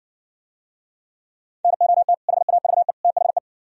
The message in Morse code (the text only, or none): NOT HAVE THE